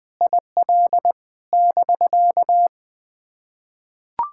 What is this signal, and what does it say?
Morse code, heard as a keyed tone, about 20 wpm, 700 Hz. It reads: IL <BK>
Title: POTA QSO Practice - 20wpm